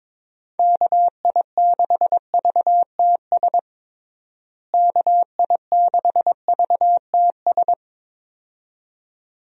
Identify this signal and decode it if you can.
Morse code, heard as a keyed tone, about 22 wpm, 700 Hz. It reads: KI6VTS KI6VTS
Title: USA Call Signs - 6 Characters - Repeated 2x - 22wpm